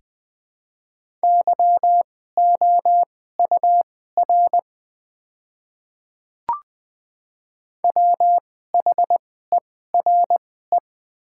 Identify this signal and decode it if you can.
Morse code, heard as a keyed tone, about 20 wpm, spaced as for 14 wpm, 700 Hz. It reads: YOUR WHERE
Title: 2x Character Spacing - Top 200 Words - 20wpm